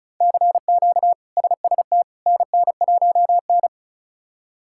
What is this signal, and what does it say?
Morse code, heard as a keyed tone, about 34 wpm, 700 Hz. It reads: CQ SST NN1N
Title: K1USN SST QSO Practice - 35wpm